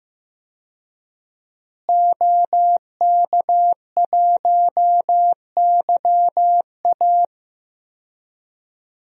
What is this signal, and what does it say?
Morse code, heard as a keyed tone, about 15 wpm, 700 Hz. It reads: OK1YA
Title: DXCC-100 Call Signs - 5 Characters - Repeated 2x - 15wpm